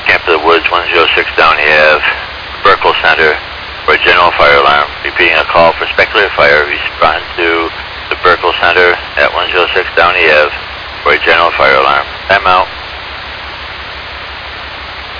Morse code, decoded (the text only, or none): none